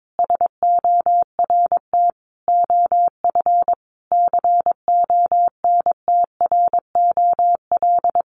Morse code, heard as a keyed tone, about 22 wpm, 700 Hz. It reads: SORT OF CONTROL
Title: Sentences from Top 1300 Words 22wpm